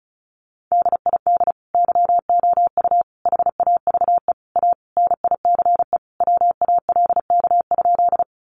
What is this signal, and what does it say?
Morse code, heard as a keyed tone, about 34 wpm, 700 Hz. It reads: DID YOU HAVE A NICE WALK?
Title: Sentences from Top 1100 Words 35wpm